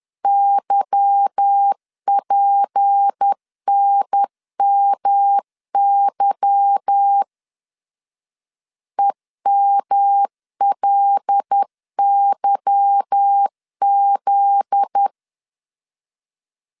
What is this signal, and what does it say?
Morse code, heard as a keyed tone, about 10 wpm, 800 Hz. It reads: YPNMY EMLYZ
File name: LEZIONI / 8 YJ45 / 8 Y 10wpm 2615